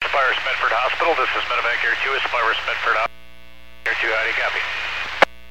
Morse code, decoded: none

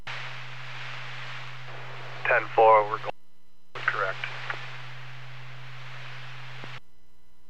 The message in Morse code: none

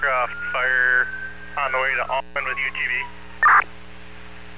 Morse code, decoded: none

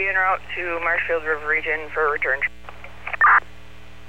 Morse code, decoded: none

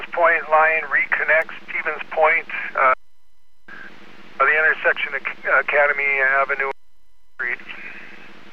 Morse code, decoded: none

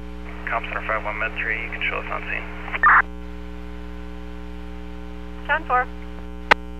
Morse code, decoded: none

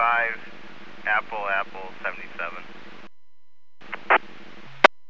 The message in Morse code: none

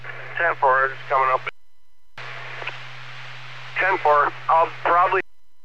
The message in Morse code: none

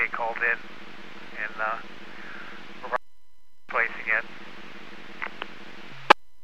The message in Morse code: none

